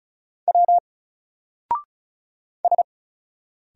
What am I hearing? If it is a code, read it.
Morse code, heard as a keyed tone, about 35 wpm, 700 Hz. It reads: W S
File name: Single-Letter-Number-35wpm